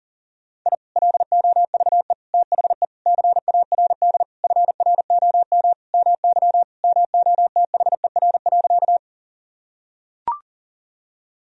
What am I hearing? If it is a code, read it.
Morse code, heard as a keyed tone, about 40 wpm, 700 Hz. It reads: I LOVE THE CARD FROM MY MOTHER.
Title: Sentences from Top 1000 Words - Review Part 4 - 40wpm